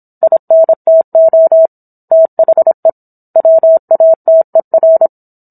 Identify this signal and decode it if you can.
Morse code, heard as a keyed tone, about 27 wpm, 650 Hz. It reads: INTO THE WATER